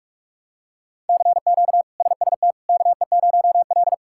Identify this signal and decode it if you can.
Morse code, heard as a keyed tone, about 46 wpm, 700 Hz. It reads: CQ SST KE0L
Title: K1USN SST QSO Practice - 45wpm